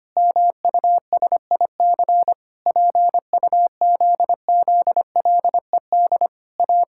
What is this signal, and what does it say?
Morse code, heard as a keyed tone, about 25 wpm, 700 Hz. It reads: MUSIC PUZZLED A
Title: Pangram Challenge v2 - 25wpm